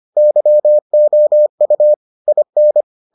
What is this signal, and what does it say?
Morse code, heard as a keyed tone, about 25 wpm, 600 Hz. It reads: YOU IN